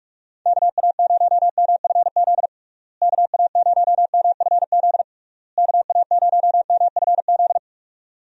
Morse code, heard as a keyed tone, about 46 wpm, 700 Hz. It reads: KA0MFZ KA0MFZ KA0MFZ
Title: USA Call Signs - 6 Characters - Repeated 3x - 45wpm